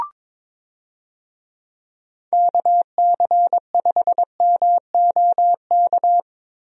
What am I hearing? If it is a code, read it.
Morse code, heard as a keyed tone, about 22 wpm, 700 Hz. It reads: KC5MOK
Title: USA Call Signs - 6 Characters - Repeated 3x - 22wpm